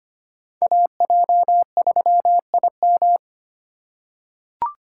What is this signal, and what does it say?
Morse code, heard as a keyed tone, about 25 wpm, 700 Hz. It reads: AJ3IM
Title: USA Call Signs - 5 Characters - 2x Character Spacing - 25wpm